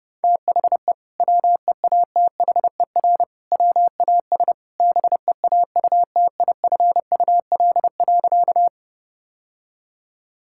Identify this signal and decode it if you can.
Morse code, heard as a keyed tone, about 32 wpm, 700 Hz. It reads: THE WEATHER WAS BEAUTIFUL.
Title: Sentences from Top 1000 Words - Review Part 5 - 30wpm